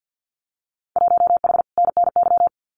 Morse code, heard as a keyed tone, about 50 wpm, 700 Hz. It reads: 1H NNY
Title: Winter Field Day Exchanges 50wpm